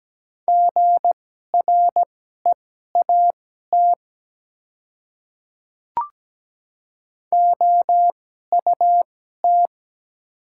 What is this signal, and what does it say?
Morse code, heard as a keyed tone, about 17 wpm, spaced as for 12 wpm, 700 Hz. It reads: GREAT OUT
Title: 2x Character Spacing - Top 200 Words - 17wpm